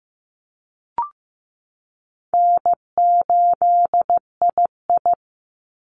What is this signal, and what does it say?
Morse code, heard as a keyed tone, about 15 wpm, 700 Hz. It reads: N8II
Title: Easy Call Signs Repeated 2x - 15wpm